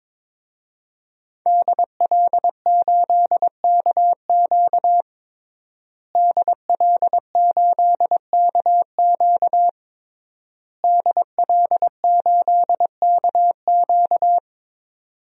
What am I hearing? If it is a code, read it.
Morse code, heard as a keyed tone, about 22 wpm, 700 Hz. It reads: DL8KQ DL8KQ DL8KQ
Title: DXCC-100 Call Signs - 5 Characters - Repeated 3x - 22wpm